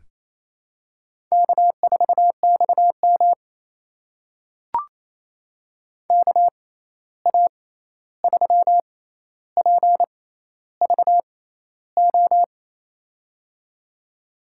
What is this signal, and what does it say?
Morse code, heard as a keyed tone, about 28 wpm, spaced as for 13 wpm, 700 Hz. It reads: K4XM KA3PVO
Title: USA Call Signs - 6x Character Spacing - 28wpm